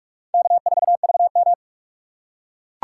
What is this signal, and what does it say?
Morse code, heard as a keyed tone, about 42 wpm, 700 Hz. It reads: K4VM